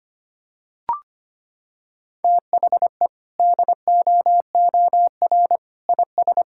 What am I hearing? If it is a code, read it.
Morse code, heard as a keyed tone, about 25 wpm, 700 Hz. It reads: THE DOOR IS
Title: Sentences from Top 1600 Words 25wpm